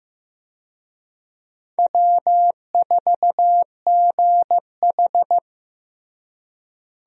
Morse code, heard as a keyed tone, about 15 wpm, 700 Hz. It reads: W4GH
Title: USA Call Signs - 4 Characters - Repeated 3x - 15wpm